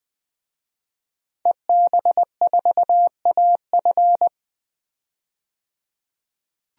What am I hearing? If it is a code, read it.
Morse code, heard as a keyed tone, about 20 wpm, 700 Hz. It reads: EB4AF
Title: DXCC-100 Call Signs - 5 Characters - 20wpm